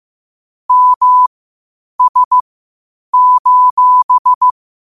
Morse code, heard as a keyed tone, about 15 wpm, spaced as for 8 wpm, 1000 Hz. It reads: MS:
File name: qotdfiles / 8wpm